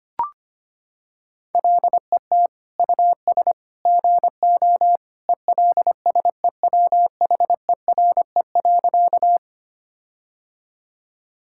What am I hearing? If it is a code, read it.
Morse code, heard as a keyed tone, about 25 wpm, 700 Hz. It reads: LET US GO ELSEWHERE.
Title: Sentences from Top 1600 Words 25wpm